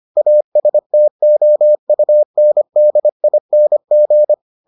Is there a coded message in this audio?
Yes: Morse code from 0.2 to 4.3 s, heard as a keyed tone, about 25 wpm, 600 Hz. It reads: ASTOUNDING